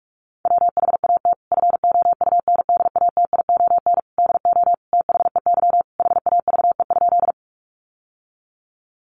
Morse code, heard as a keyed tone, about 44 wpm, 700 Hz. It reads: WHAT FOUNDATION DO THEY HAVE?
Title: Sentences from Top 1200 Words 45wpm